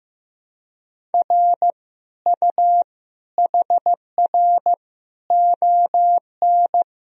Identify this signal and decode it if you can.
Morse code, heard as a keyed tone, about 15 wpm, 700 Hz. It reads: R U HR ON
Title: Sentences from Top 400 Words and Abbreviations 15wpm